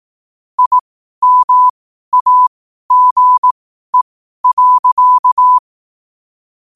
Morse code, heard as a keyed tone, about 18 wpm, spaced as for 12 wpm, 1000 Hz. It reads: IMAGE.
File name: qotdfiles / 12wpm